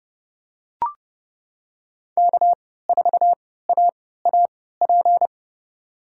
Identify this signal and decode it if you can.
Morse code, heard as a keyed tone, about 32 wpm, spaced as for 17 wpm, 700 Hz. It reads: K4AAP
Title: USA Call Signs - 5 Characters - 3x Character Spacing - 30wpm